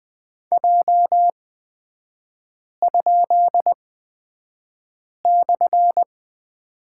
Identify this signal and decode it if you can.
Morse code, heard as a keyed tone, about 20 wpm, spaced as for 5 wpm, 690 Hz. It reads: J?/
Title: 236 Lesson 38e - Letter q - Three Characters 20wpm